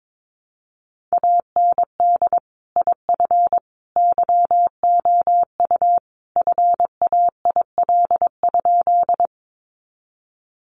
Morse code, heard as a keyed tone, about 22 wpm, 700 Hz. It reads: AND IF YOU FAIL?
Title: Sentences from Top 1100 Words 22wpm